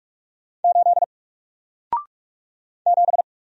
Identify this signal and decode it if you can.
Morse code, heard as a keyed tone, about 44 wpm, 700 Hz. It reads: 8 7
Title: Single Character (Letter:Number:Punctuation:Prosign) Alt 45wpm